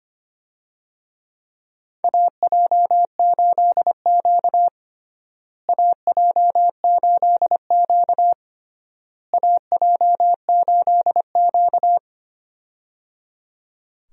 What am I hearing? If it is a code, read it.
Morse code, heard as a keyed tone, about 25 wpm, 700 Hz. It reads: AJ8Q AJ8Q AJ8Q
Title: USA Call Signs - 4 Characters - Repeated 3x - 25wpm